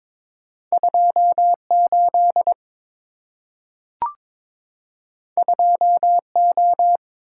Morse code, heard as a keyed tone, about 22 wpm, 700 Hz. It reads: 28 2O
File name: practice / 22wpm / Two Letter-Number Combinations - 22wpm